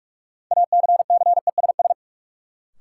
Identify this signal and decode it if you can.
Morse code, heard as a keyed tone, about 46 wpm, 700 Hz. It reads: ACCESS
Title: Sentences from Top 1000 Words - Review Part 4 - New Word Drill - 45wpm